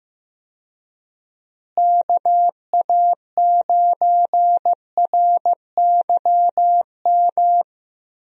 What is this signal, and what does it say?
Morse code, heard as a keyed tone, about 16 wpm, 700 Hz. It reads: KA9RYM
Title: USA Call Signs - 6 Characters - Repeated 3x - 15wpm